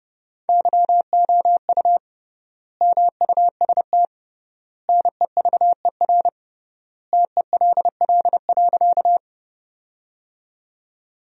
Morse code, heard as a keyed tone, about 30 wpm, 700 Hz. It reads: YOU MUST NEVER TELL.
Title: 3x Word-Spacing - Sentences from Top 300 Words - 30wpm